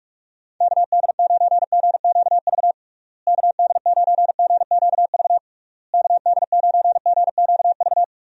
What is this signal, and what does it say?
Morse code, heard as a keyed tone, about 47 wpm, 700 Hz. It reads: KD9GQV KD9GQV KD9GQV
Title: USA Call Signs - 6 Characters - Repeated 3x - 45wpm